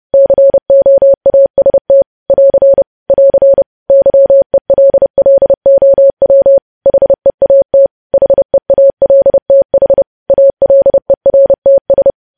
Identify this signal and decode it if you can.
Morse code, heard as a keyed tone, about 30 wpm, 550 Hz. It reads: COAST <AR> <AR> YELLOW HEAT HEALTH ALERTS
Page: news headlines in morse code at 30 wpm